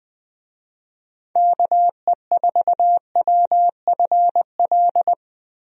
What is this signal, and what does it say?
Morse code, heard as a keyed tone, about 21 wpm, 700 Hz. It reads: KE4WFL
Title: USA Call Signs - 6 Characters - Repeated 3x - 20wpm